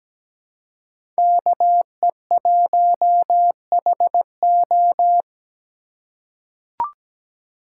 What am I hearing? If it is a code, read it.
Morse code, heard as a keyed tone, about 17 wpm, 700 Hz. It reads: KE1HO
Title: USA Call Signs - 5 Characters - 4x Character Spacing - 17wpm